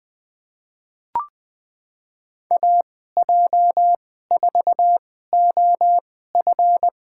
Morse code, heard as a keyed tone, about 20 wpm, spaced as for 14 wpm, 700 Hz. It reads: AJ4OF
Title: USA Call Signs - 5 Characters - 2x Character Spacing - 20wpm